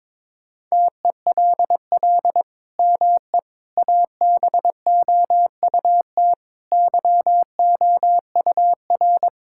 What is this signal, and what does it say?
Morse code, heard as a keyed tone, about 22 wpm, 700 Hz. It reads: TELL ME ABOUT YOUR
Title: Sentences from Top 1000 Words - Review Part 5 - 22wpm